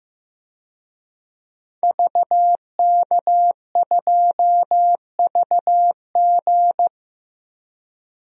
Morse code, heard as a keyed tone, about 16 wpm, 700 Hz. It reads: VK2VG
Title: DXCC-100 Call Signs - 5 Characters - 15wpm